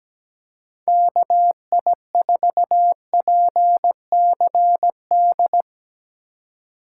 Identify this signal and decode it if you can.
Morse code, heard as a keyed tone, about 19 wpm, 700 Hz. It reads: KI4PCD
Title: Speed Racing - USA 2x3 Call Signs 17wpm